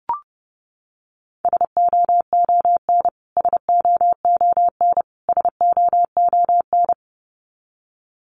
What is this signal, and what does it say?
Morse code, heard as a keyed tone, about 30 wpm, 700 Hz. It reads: SOON SOON SOON
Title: Top 500 Words Repeated 30wpm